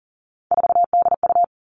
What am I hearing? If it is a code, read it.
Morse code, heard as a keyed tone, about 40 wpm, 700 Hz. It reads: 4DU
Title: All Cut Numbers - 3-Digit 40wpm